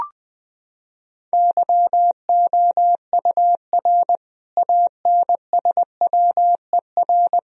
Morse code, heard as a keyed tone, about 20 wpm, 700 Hz. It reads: YOUR ANSWER
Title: Sentences from Top 1000 Words - Review Part 3 - 20wpm